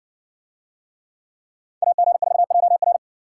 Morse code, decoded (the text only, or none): AG4JR